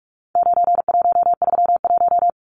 Morse code, heard as a keyed tone, about 44 wpm, 700 Hz. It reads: 9131
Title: Numbers - 4-Digit 45wpm